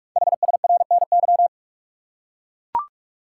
Morse code, heard as a keyed tone, about 44 wpm, 700 Hz. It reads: HSRNY
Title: Code Groups - 5 Letters - 45wpm